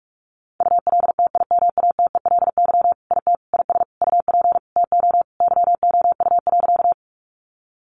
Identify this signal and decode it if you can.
Morse code, heard as a keyed tone, about 44 wpm, 700 Hz. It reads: ULTIMATELY IT IS UP TO YOU.
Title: Sentences from Top 1200 Words 45wpm